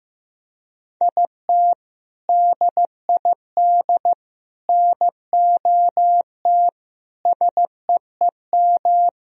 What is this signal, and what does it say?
Morse code, heard as a keyed tone, about 15 wpm, 700 Hz. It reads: IT DID NOT SEEM
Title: Sentences from Top 1000 Words - Review Part 1 - 15wpm